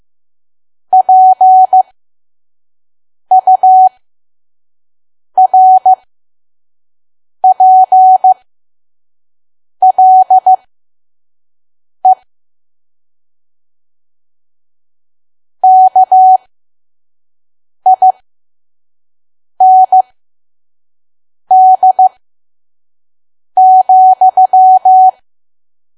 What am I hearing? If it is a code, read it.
Morse code, heard as a keyed tone, about 15 wpm, spaced as for 5 wpm, 750 Hz. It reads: PURPLE KIND,
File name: Archive / 5 WPM files / PM